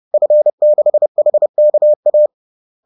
Morse code, heard as a keyed tone, about 30 wpm, 600 Hz. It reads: F6HKA
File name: 30wpm-4